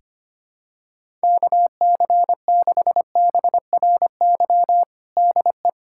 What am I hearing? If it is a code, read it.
Morse code, heard as a keyed tone, about 25 wpm, 700 Hz. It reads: KC6BRY DE
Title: USA Callsign DE Callsign - 6 Characters - 25wpm